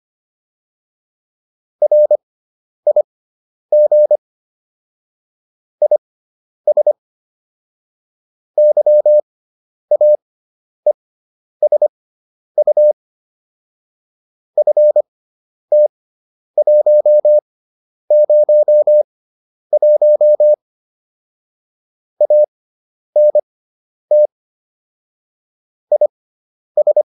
Morse code, heard as a keyed tone, about 25 wpm, spaced as for 10 wpm, 600 Hz. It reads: RIG IS YAESU FT101 ANT IS
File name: QSO405 10